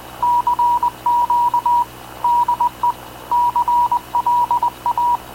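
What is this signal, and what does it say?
Morse code, heard as a keyed tone, about 19 wpm, 960 Hz. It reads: CQ DE CLA
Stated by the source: Havana Radio / CLA sending Spanish press (PX) in Morse, recorded by Radio Officer Paul Dumesnil onboard CNAV QUEST/CZDO.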